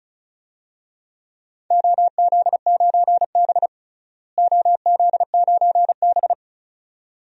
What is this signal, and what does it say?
Morse code, heard as a keyed tone, about 36 wpm, 700 Hz. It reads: OZ9B OZ9B
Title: DXCC-100 Call Signs - 4 Characters - Repeated 2x - 35wpm